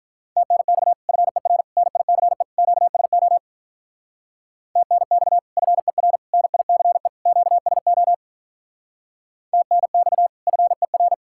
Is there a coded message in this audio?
Yes: Morse code from 0.4 to 11.1 s, heard as a keyed tone, about 47 wpm, 700 Hz. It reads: TNX FER NICE QSO TNX FER NICE QSO TNX FER